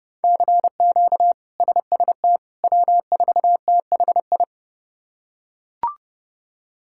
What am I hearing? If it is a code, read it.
Morse code, heard as a keyed tone, about 30 wpm, 700 Hz. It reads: CQ SST W4THI
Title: K1USN SST QSO Practice - 30wpm